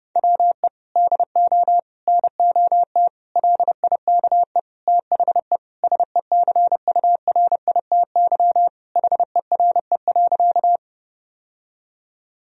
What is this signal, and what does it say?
Morse code, heard as a keyed tone, about 30 wpm, 700 Hz. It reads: WE DO NOT LIKE THE SECURITY HERE.
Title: Sentences from Top 1000 Words - Review Part 4 - 30wpm